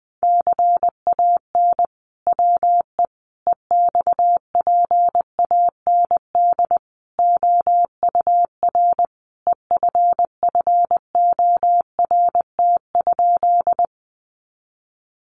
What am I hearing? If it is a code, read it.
Morse code, heard as a keyed tone, about 20 wpm, 700 Hz. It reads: CAN WE EXPAND OUR EFFORT?
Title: Sentences from Top 1400 Words 20wpm